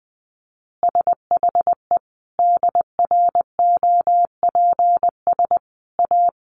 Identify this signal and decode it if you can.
Morse code, heard as a keyed tone, about 20 wpm, 700 Hz. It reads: SHE DROPS A
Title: Additional Easy Phrases 20wpm